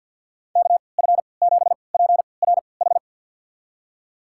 Morse code, heard as a keyed tone, about 51 wpm, spaced as for 28 wpm, 700 Hz. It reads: KF7PRH
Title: USA Call Signs - 6 Characters - 3x Character Spacing - 50wpm